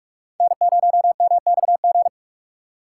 Morse code, heard as a keyed tone, about 46 wpm, 700 Hz. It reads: N0MXG
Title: USA Call Signs - 5 Characters - 6x Character Spacing - 45wpm